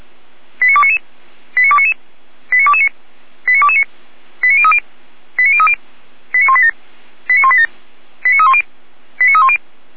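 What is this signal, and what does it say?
Five-tone selective call (ZVEI), heard as quick runs of five tones, tones of about 70 ms. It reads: 89299 89299 89229 89229 89030 89030 89187 89187 89310 89310
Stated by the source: ZVEI five-tone call received on SDR.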